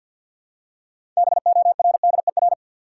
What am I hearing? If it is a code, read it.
Morse code, heard as a keyed tone, about 52 wpm, 700 Hz. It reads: BORDER
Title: Six Letter Words - 50wpm